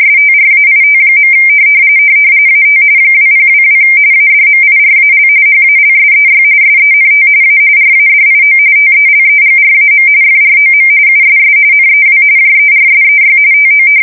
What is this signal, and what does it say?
RTTY, about 45.45 baud. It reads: STRAIT. AT 2022 ADMIRAL HALSEY SENT A RADIO TELEGRAM FROM USS NEW JERSEY TO
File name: fsk170 / battleshipnj